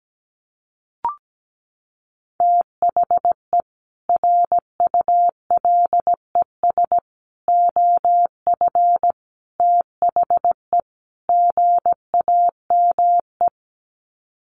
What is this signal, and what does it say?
Morse code, heard as a keyed tone, about 17 wpm, 700 Hz. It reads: THE RULES OF THE GAME
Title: Sets of 5 Words Encore2 17wpm